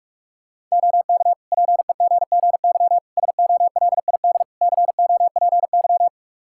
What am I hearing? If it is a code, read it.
Morse code, heard as a keyed tone, about 42 wpm, 700 Hz. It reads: OK PEGGY SOLID COPY
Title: Practice QSO - Exchange 3 - More Info - 45wpm